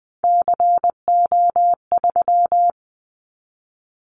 Morse code, heard as a keyed tone, about 20 wpm, 700 Hz. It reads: CO3
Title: Callsign Prefixes for Top 100 DXCC - 20wpm